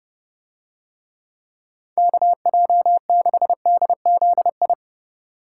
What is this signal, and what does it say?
Morse code, heard as a keyed tone, about 31 wpm, 700 Hz. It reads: KJ6DZI